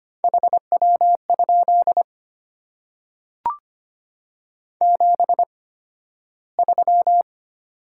Morse code, heard as a keyed tone, about 25 wpm, spaced as for 13 wpm, 700 Hz. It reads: HW? 73
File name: Top 90 QSO Elements - 8x Character Spacing - 25wpm